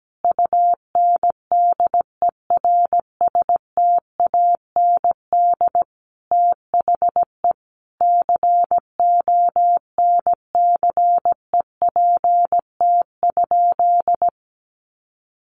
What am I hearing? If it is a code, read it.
Morse code, heard as a keyed tone, about 17 wpm, 700 Hz. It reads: UNDERSTAND THE CONCEPT?
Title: Sentences from Top 1100 Words 17wpm